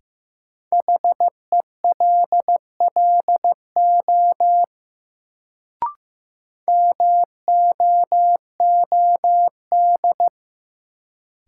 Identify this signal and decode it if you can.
Morse code, heard as a keyed tone, about 15 wpm, 700 Hz. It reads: HELLO MOOD
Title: Sentences from Top 1100 Words - New Word Drill - 15wpm